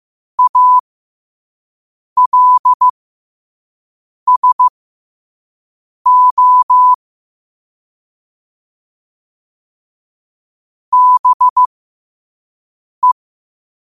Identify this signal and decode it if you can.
Morse code, heard as a keyed tone, about 15 wpm, spaced as for 5 wpm, 1000 Hz. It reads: ALSO BE
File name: qotdfiles / 5wpm